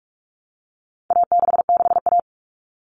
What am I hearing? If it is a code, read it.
Morse code, heard as a keyed tone, about 44 wpm, 700 Hz. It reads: A66A
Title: All Cut Numbers - 4-Digit 45wpm